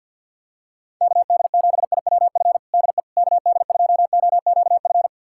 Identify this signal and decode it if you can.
Morse code, heard as a keyed tone, about 50 wpm, 700 Hz. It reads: KD7IWF DE KD2OQF
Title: USA Callsign DE Callsign - 6 Characters - 50wpm